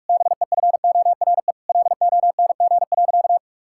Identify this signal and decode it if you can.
Morse code, heard as a keyed tone, about 41 wpm, 700 Hz. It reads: BEFORE LONG.
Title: Sentences from Top 1600 Words 45wpm